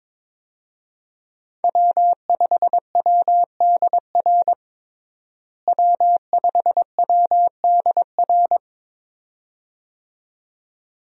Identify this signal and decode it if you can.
Morse code, heard as a keyed tone, about 22 wpm, 700 Hz. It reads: W5WDR W5WDR
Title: USA Call Signs - 5 Characters - Repeated 2x - 22wpm